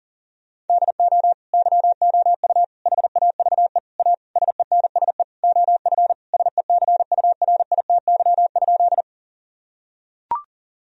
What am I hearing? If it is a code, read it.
Morse code, heard as a keyed tone, about 37 wpm, 700 Hz. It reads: DO YOU HAVE A SENSE OF SECURITY?
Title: Sentences from Top 1000 Words - Review Part 4 - 40wpm